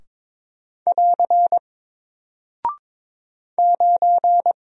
Morse code, heard as a keyed tone, about 22 wpm, 700 Hz. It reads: <AR> 9
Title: Single Character (Letter:Number:Punctuation:Prosign) Alt 22wpm